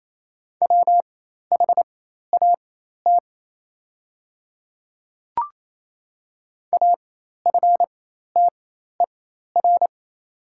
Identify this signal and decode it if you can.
Morse code, heard as a keyed tone, about 28 wpm, spaced as for 13 wpm, 700 Hz. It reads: WHAT AFTER
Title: 4x Character Spacing - Top 200 Words - 28wpm